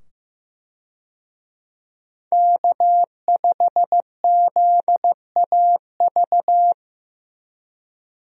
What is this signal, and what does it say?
Morse code, heard as a keyed tone, about 15 wpm, 700 Hz. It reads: K5ZAV
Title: USA Call Signs - 5 Characters - 15wpm